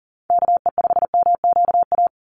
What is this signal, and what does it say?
Morse code, heard as a keyed tone, about 40 wpm, 700 Hz. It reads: KE5MQA
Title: USA Call Signs 40wpm